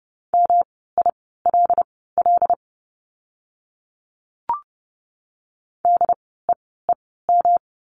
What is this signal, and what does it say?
Morse code, heard as a keyed tone, about 30 wpm, spaced as for 17 wpm, 700 Hz. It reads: MILL DEEM
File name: course / 30:10wpm / 108 Lesson 18f - Letter m - Words 30wpm